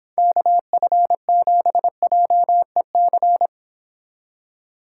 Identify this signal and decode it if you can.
Morse code, heard as a keyed tone, about 29 wpm, 700 Hz. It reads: KF7JEC